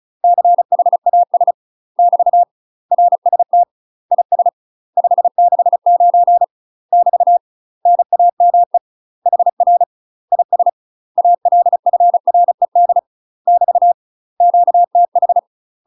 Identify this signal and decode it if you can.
Morse code, heard as a keyed tone, about 34 wpm, 700 Hz. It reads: CHAS <BT> RST IS 569 <BT> NAME HR IS ALFRED <BT> QTH
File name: UK MASTER QSOs TRAINING / 35WPM / UK MASTER QSOs 0007